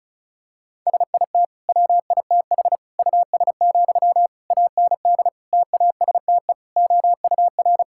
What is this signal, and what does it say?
Morse code, heard as a keyed tone, about 36 wpm, 700 Hz. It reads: SIT WITH US, AND TASTE OUR
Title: Sentences from Top 1600 Words 35wpm